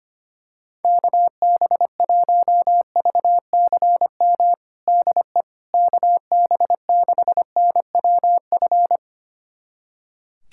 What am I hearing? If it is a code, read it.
Morse code, heard as a keyed tone, about 25 wpm, 700 Hz. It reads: KB1VCM DE KB6NWF
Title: USA Callsign DE Callsign - 6 Characters - 25wpm